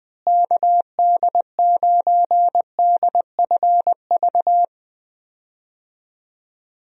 Morse code, heard as a keyed tone, about 20 wpm, 700 Hz. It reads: KD9DFV
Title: USA Call Signs - 6 Characters - Repeated 3x - 20wpm